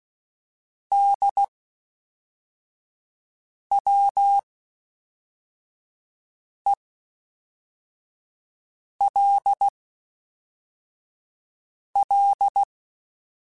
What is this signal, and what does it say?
Morse code, heard as a keyed tone, about 16 wpm, 800 Hz. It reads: D W E L L
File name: Special INGs / Special INGs-mp3s / Fast ing @10